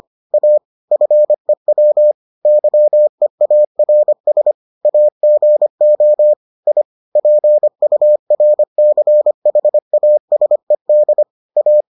Morse code, heard as a keyed tone, about 25 wpm, 600 Hz. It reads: A FEW YEARS AGO I PURCHASED A